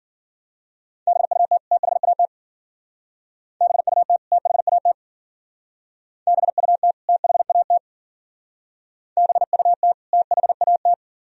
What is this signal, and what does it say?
Morse code, heard as a keyed tone, about 51 wpm, 700 Hz. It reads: BUT THAT BUT THAT BUT THAT BUT THAT